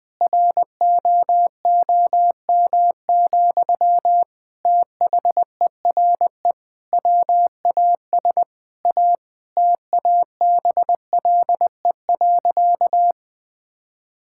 Morse code, heard as a keyed tone, about 20 wpm, 700 Hz. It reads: ROOM, THERE WAS A TABLE.
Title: Sentences from Top 1000 Words - Review Part 6 - 20wpm